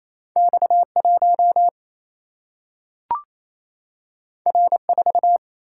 Morse code, heard as a keyed tone, about 28 wpm, 700 Hz. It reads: X 1 R 4
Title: Single-Letter Single-Number 28wpm